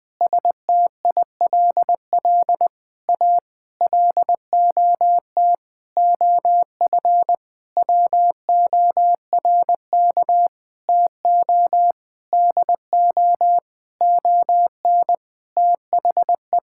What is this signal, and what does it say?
Morse code, heard as a keyed tone, about 19 wpm, 700 Hz. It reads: STILL A LOT OF WORK TO DO ON THE
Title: Sentences with 9-Word Phrases - Spoken Beginning - 20wpm